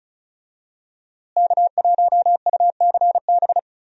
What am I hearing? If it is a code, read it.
Morse code, heard as a keyed tone, about 34 wpm, 700 Hz. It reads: K1UCB
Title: USA Call Signs - 5 Characters - Repeated 2x - 35wpm